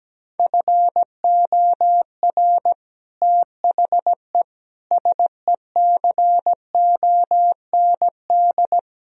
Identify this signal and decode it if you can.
Morse code, heard as a keyed tone, about 17 wpm, 700 Hz. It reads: FOR THE SECOND
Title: Sentences from Top 1000 Words - Review Part 1 - 17wpm